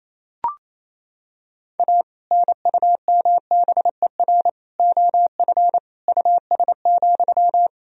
Morse code, heard as a keyed tone, about 28 wpm, 700 Hz. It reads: A NUMBER OF US,
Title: Sentences from Top 1000 Words - Review Part 1 - 28wpm